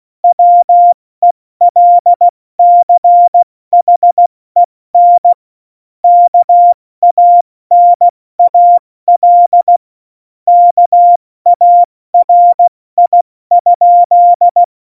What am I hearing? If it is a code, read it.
Morse code, heard as a keyed tone, about 16 wpm, 700 Hz. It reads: WELCHEN KANAL KARI?
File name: Morsekurs / Witze / Witz 21. November 2017, 70 F80